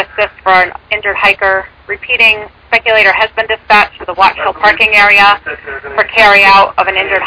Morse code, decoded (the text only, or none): none